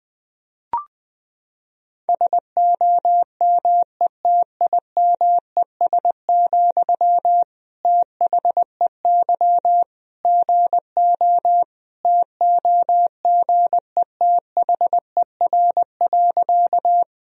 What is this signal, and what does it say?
Morse code, heard as a keyed tone, about 20 wpm, 700 Hz. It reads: SOMETIMES, THEY GO TOGETHER.
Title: Sentences from Top 1000 Words - Review Part 3 - 20wpm